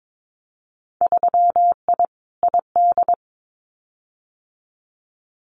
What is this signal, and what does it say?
Morse code, heard as a keyed tone, about 22 wpm, 700 Hz. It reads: 3I ID
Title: Winter Field Day Exchanges Encore 22wpm